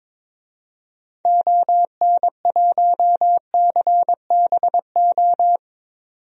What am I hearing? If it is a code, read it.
Morse code, heard as a keyed tone, about 22 wpm, 700 Hz. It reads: ON1CBO
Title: DXCC-100 Call Signs - 6 Characters - 22wpm